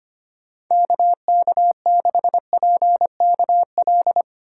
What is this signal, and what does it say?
Morse code, heard as a keyed tone, about 25 wpm, 700 Hz. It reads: KK6PKL
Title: USA Call Signs - 6 Characters - 4x Character Spacing - 25wpm